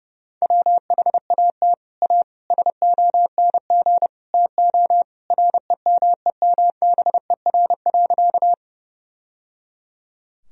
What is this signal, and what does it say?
Morse code, heard as a keyed tone, about 29 wpm, 700 Hz. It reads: WHAT A SONG TO REMEMBER.
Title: Sentences from Top 1000 Words - Review Part 2 - 30wpm